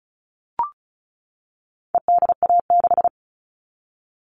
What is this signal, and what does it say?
Morse code, heard as a keyed tone, about 34 wpm, 700 Hz. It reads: EDA6
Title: All Cut Numbers - 4-Digit 35wpm